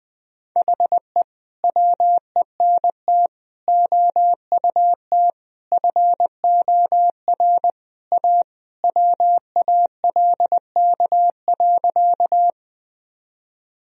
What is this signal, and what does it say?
Morse code, heard as a keyed tone, about 20 wpm, 700 Hz. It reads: HE WENT OUT FOR A WALK.
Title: Sentences from Top 1000 Words - Review Part 1 - 20wpm